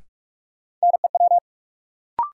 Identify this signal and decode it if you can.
Morse code, heard as a keyed tone, about 46 wpm, 700 Hz. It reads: NEW
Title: Sets of 2 Words - New Word Drill - 45wpm